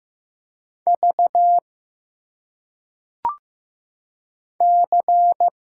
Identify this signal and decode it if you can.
Morse code, heard as a keyed tone, about 15 wpm, 700 Hz. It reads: V C
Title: Single Letters - Phonetic Pronunciation 15wpm